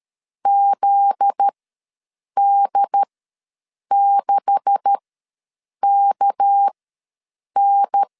Morse code, heard as a keyed tone, about 13 wpm, spaced as for 7 wpm, 800 Hz. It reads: ZD6KN
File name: LEZIONI / 9 286 ws 12wpm 3351